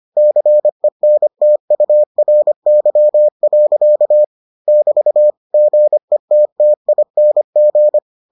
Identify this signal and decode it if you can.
Morse code, heard as a keyed tone, about 25 wpm, 600 Hz. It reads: CENTURY. <BT> GETTING